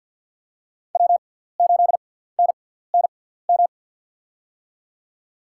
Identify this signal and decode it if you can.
Morse code, heard as a keyed tone, about 52 wpm, spaced as for 17 wpm, 700 Hz. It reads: W8NNM
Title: USA Call Signs - 5 Characters - 6x Character Spacing - 50wpm